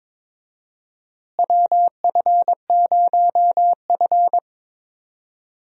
Morse code, heard as a keyed tone, about 22 wpm, 700 Hz. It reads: WF0F
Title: USA Call Signs - 4 Characters - Repeated 2x - 22wpm